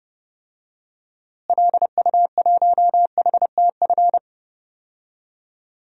Morse code, heard as a keyed tone, about 30 wpm, 700 Hz. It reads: LU1HTF
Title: DXCC-100 Call Signs - 6 Characters - Repeated 2x - 30wpm